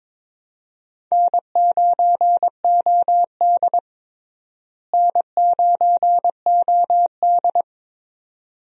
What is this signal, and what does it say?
Morse code, heard as a keyed tone, about 22 wpm, 700 Hz. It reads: N9OD N9OD
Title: USA Call Signs - 4 Characters - Repeated 2x - 22wpm